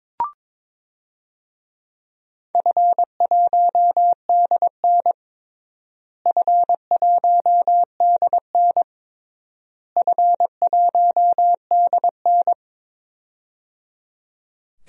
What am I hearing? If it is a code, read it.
Morse code, heard as a keyed tone, about 22 wpm, 700 Hz. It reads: F1DN F1DN F1DN
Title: DXCC-100 Call Signs - 4 Characters - Repeated 3x - 22wpm